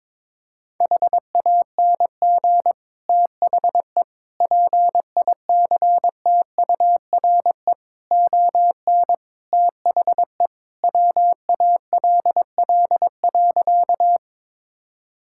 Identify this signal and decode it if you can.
Morse code, heard as a keyed tone, about 22 wpm, 700 Hz. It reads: HANG THE PICTURE ON THE WALL.
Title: Sentences from Top 1600 Words 22wpm